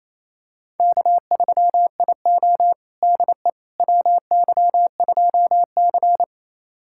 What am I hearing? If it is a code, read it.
Morse code, heard as a keyed tone, about 28 wpm, 700 Hz. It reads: K3IO DE WY2C
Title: USA Callsign DE Callsign - 4 Characters - 28wpm